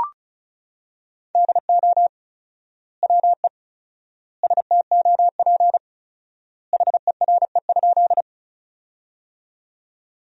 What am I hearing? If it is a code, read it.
Morse code, heard as a keyed tone, about 36 wpm, 700 Hz. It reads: DO WE STOP HERE?
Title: 4x Word-Spacing - Sentences from Top 300 Words - 35wpm